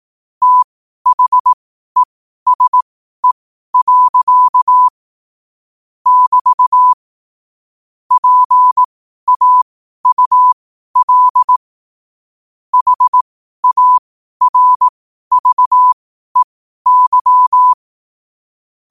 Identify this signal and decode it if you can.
Morse code, heard as a keyed tone, about 18 wpm, spaced as for 12 wpm, 1000 Hz. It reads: THESE. <BT> PAUL HARVEY
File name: qotdfiles / 12wpm